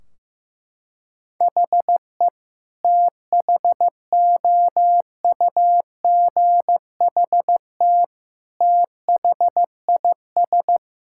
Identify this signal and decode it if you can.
Morse code, heard as a keyed tone, about 16 wpm, 700 Hz. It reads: HE THOUGHT THIS
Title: Sentences from Top 1600 Words 15wpm